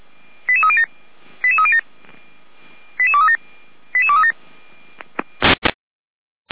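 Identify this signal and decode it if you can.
Five-tone selective call (ZVEI), heard as quick runs of five tones, tones of about 70 ms. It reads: 88307 88307 88237 88237
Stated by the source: ZVEI five-tone selective call received on an SDR.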